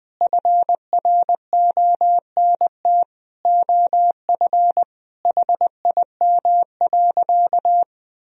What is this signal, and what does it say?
Morse code, heard as a keyed tone, about 20 wpm, 700 Hz. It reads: FRONT OF HIM.
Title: Sentences from Top 1000 Words - Review Part 2 - 20wpm